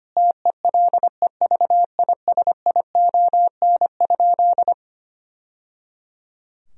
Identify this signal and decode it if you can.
Morse code, heard as a keyed tone, about 25 wpm, 700 Hz. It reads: TELEVISION?
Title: Sentences from Top 1000 Words - Review Part 4 - 25wpm